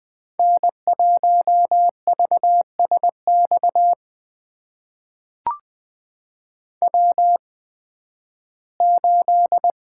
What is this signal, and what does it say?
Morse code, heard as a keyed tone, about 20 wpm, spaced as for 13 wpm, 700 Hz. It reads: N1VSX W8
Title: USA Call Signs - 5 Characters - 8x Character Spacing - 20wpm